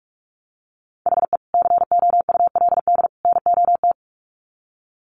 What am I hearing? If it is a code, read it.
Morse code, heard as a keyed tone, about 44 wpm, 700 Hz. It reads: HE COULD NOT
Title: Sets of 3 Words 45wpm